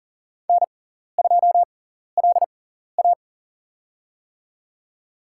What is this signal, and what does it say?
Morse code, heard as a keyed tone, about 40 wpm, spaced as for 14 wpm, 700 Hz. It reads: N2LA
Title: USA Call Signs - 4 Characters - 6x Character Spacing - 40wpm